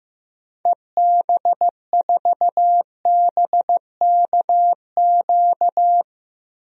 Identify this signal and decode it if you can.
Morse code, heard as a keyed tone, about 15 wpm, 700 Hz. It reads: EB4BKQ
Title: DXCC-100 Call Signs - 6 Characters - Repeated 2x - 15wpm